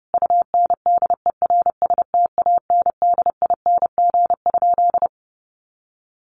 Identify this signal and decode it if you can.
Morse code, heard as a keyed tone, about 30 wpm, 700 Hz. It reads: UNDERSTANDING?
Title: Sentences from Top 1500 Words 30wpm